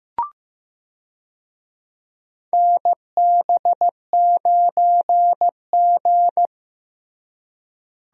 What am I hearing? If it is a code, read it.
Morse code, heard as a keyed tone, about 15 wpm, 700 Hz. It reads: NB9G
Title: USA Call Signs - 4 Characters - Repeated 3x - 15wpm